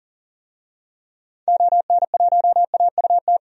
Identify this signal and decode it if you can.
Morse code, heard as a keyed tone, about 40 wpm, 700 Hz. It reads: ON1AUT
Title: DXCC-100 Call Signs - 6 Characters - 40wpm